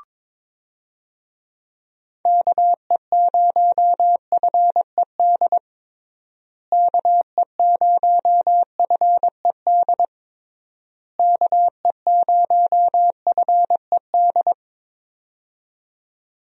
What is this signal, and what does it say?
Morse code, heard as a keyed tone, about 22 wpm, 700 Hz. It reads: KE0FED KE0FED KE0FED
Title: USA Call Signs - 6 Characters - Repeated 3x - 22wpm